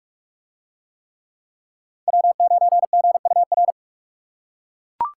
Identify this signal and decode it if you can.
Morse code, heard as a keyed tone, about 41 wpm, 700 Hz. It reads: W9GUR